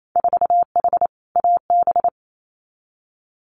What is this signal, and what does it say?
Morse code, heard as a keyed tone, about 28 wpm, 690 Hz. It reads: 4H AB
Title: Winter Field Day Exchanges Encore 28wpm